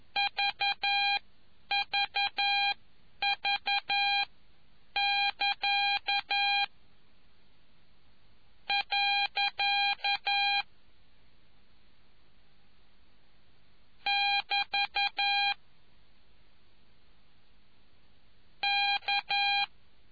Morse code, heard as a keyed tone, about 11 wpm, spaced as for 8 wpm, 2400 Hz. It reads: VVV<KA> . <BT> K